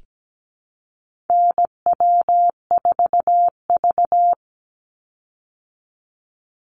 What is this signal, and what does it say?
Morse code, heard as a keyed tone, about 17 wpm, 700 Hz. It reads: NW4V
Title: Call Signs Repeated 3x - 17wpm